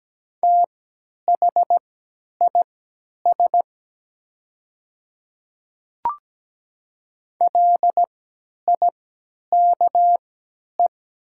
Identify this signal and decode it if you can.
Morse code, heard as a keyed tone, about 17 wpm, spaced as for 10 wpm, 700 Hz. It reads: THIS LIKE
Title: 3x Character Spacing - Top 100 Words - 17wpm